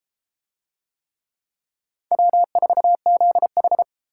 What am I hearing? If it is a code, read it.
Morse code, heard as a keyed tone, about 30 wpm, 700 Hz. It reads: W4ZH